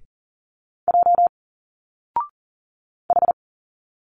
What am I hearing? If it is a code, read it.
Morse code, heard as a keyed tone, about 40 wpm, 700 Hz. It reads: J H
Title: Single Letters 40wpm